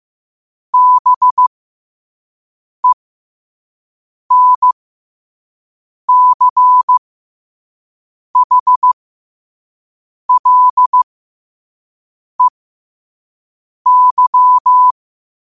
Morse code, heard as a keyed tone, about 15 wpm, spaced as for 5 wpm, 1000 Hz. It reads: BENCHLEY
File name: qotdfiles / 5wpm